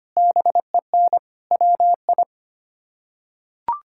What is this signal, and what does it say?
Morse code, heard as a keyed tone, about 25 wpm, 700 Hz. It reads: BEN WI
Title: K1USN SST QSO Practice - 25wpm